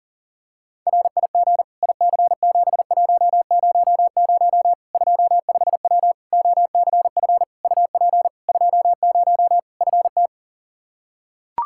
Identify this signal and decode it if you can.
Morse code, heard as a keyed tone, about 40 wpm, 700 Hz. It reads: RIG IC7100 25W OCF UP 20 FT
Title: K1USN SST QSO Practice 40wpm